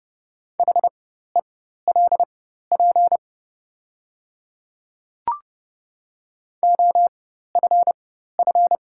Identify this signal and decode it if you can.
Morse code, heard as a keyed tone, about 30 wpm, spaced as for 14 wpm, 700 Hz. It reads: HELP OFF
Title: 4x Character Spacing - Top 200 Words - 30wpm